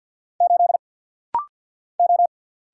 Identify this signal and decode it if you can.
Morse code, heard as a keyed tone, about 48 wpm, 700 Hz. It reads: 8 O
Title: Single Letter-Number - Rapid-Fire 50wpm